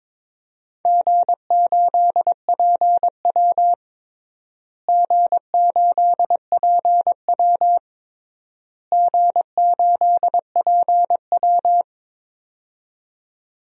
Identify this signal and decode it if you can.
Morse code, heard as a keyed tone, about 22 wpm, 700 Hz. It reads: G8PW G8PW G8PW
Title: DXCC-100 Call Signs - 4 Characters - Repeated 3x - 22wpm